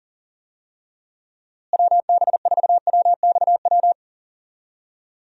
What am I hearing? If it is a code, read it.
Morse code, heard as a keyed tone, about 40 wpm, 700 Hz. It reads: WB4WXW
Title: USA Call Signs - 6 Characters - Repeated 2x - 40wpm